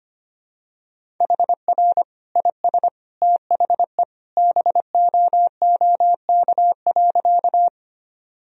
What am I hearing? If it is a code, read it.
Morse code, heard as a keyed tone, about 25 wpm, 700 Hz. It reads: HR IS THE BOOK.
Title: Sentences from Top 300 Words and Abbreviations 25wpm